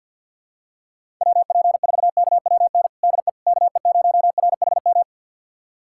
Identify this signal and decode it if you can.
Morse code, heard as a keyed tone, about 52 wpm, 700 Hz. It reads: WP4KWN DE KE0RHM
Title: USA Callsign DE Callsign - 6 Characters - 50wpm